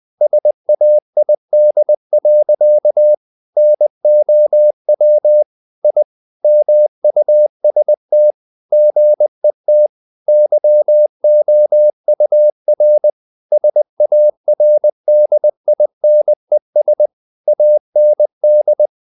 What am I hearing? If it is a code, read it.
Morse code, heard as a keyed tone, about 20 wpm, 600 Hz. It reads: SAID. NOW I MUST GET YOUR SARDINES AND
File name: the old man and the sea / the old man and the sea part 1 20wpm